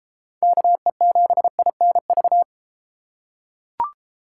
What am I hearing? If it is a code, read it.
Morse code, heard as a keyed tone, about 30 wpm, 700 Hz. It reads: KE7INV